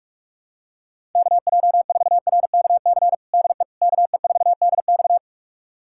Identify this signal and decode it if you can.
Morse code, heard as a keyed tone, about 46 wpm, 700 Hz. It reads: KJ4RKC DE KE4DX
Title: USA Callsign DE Callsign 45wpm